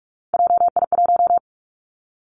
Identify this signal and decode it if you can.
Morse code, heard as a keyed tone, about 46 wpm, 700 Hz. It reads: JI1
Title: Three Challenging Letter-Number Combinations 45wpm